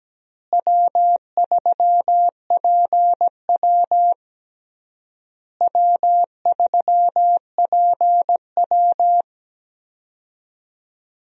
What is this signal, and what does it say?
Morse code, heard as a keyed tone, about 17 wpm, 700 Hz. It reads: W3PW W3PW
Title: USA Call Signs - 4 Characters - Repeated 3x - 17wpm